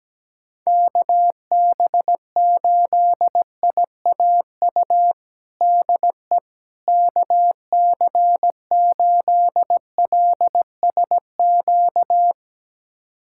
Morse code, heard as a keyed tone, about 17 wpm, 700 Hz. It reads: KB8IAU DE KC8LSQ
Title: USA Callsign DE Callsign - 6 Characters - 17wpm